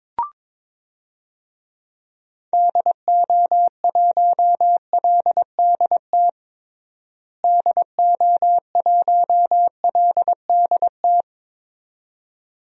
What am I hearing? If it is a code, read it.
Morse code, heard as a keyed tone, about 21 wpm, 700 Hz. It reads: DO1LDT DO1LDT
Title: DXCC-100 Call Signs - 6 Characters - Repeated 2x - 22wpm